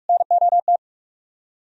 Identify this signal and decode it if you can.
Morse code, heard as a keyed tone, about 62 wpm, spaced as for 53 wpm, 700 Hz. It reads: NOT
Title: Speed-Racing - Sets of 2 Words 45wpm